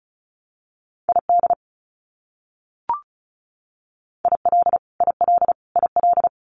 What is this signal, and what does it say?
Morse code, heard as a keyed tone, about 34 wpm, 700 Hz. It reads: ID IL IL IL
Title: US States Repeated 3x 35wpm